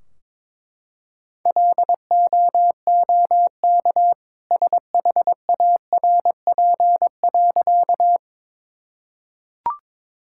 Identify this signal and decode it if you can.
Morse code, heard as a keyed tone, about 21 wpm, 700 Hz. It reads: LOOK SHARP.
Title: Sentences from Top 1600 Words 22wpm